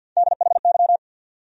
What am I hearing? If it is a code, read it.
Morse code, heard as a keyed tone, about 53 wpm, 700 Hz. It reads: DHY
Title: USA Call Sign Postfixes - 3 Letters 50wpm